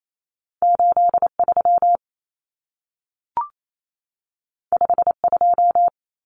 Morse code, heard as a keyed tone, about 28 wpm, 700 Hz. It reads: 83 52
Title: Common Cut Numbers - 2-Digit 28wpm